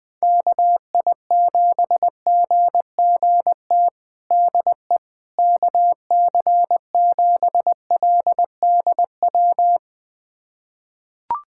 Morse code, heard as a keyed tone, about 21 wpm, 700 Hz. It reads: KI7GGT DE KC7LDW
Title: USA Callsign DE Callsign - 6 Characters - 20wpm